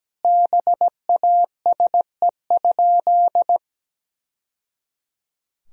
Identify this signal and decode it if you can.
Morse code, heard as a keyed tone, about 17 wpm, 700 Hz. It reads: BASE?
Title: Sentences from Top 1000 Words - Review Part 1 - 17wpm